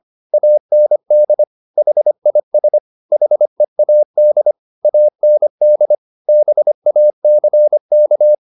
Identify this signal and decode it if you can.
Morse code, heard as a keyed tone, about 25 wpm, 600 Hz. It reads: AND HIS HEAD AND BACK
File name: the old man and the sea part 1 25wpm